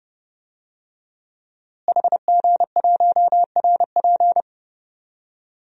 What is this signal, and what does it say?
Morse code, heard as a keyed tone, about 30 wpm, 700 Hz. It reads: HG1RP
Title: DXCC-100 Call Signs - 5 Characters - Repeated 3x - 30wpm